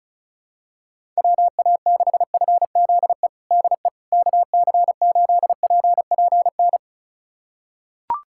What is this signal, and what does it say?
Morse code, heard as a keyed tone, about 34 wpm, 700 Hz. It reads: WA6FZE DE KC8PPN
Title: USA Callsign DE Callsign - 6 Characters - 35wpm